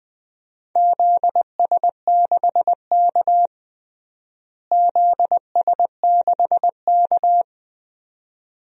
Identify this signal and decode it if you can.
Morse code, heard as a keyed tone, about 21 wpm, 700 Hz. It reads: ZS6K ZS6K
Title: DXCC-100 Call Signs - 4 Characters - Repeated 2x - 20wpm